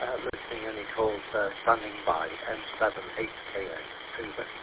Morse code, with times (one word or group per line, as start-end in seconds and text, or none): none